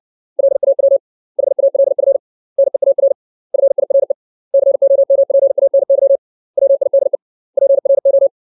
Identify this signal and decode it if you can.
Morse code, heard as a keyed tone, about 60 wpm, 550 Hz. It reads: LAW HALF NEAR FIRE COMPANY WIDE PRO
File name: 632 words / 632 Words 60WPM